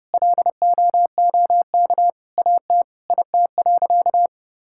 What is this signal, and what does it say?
Morse code, heard as a keyed tone, about 32 wpm, 700 Hz. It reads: LOOK AT IT.
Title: Sentences from Top 1000 Words - Review Part 5 - 30wpm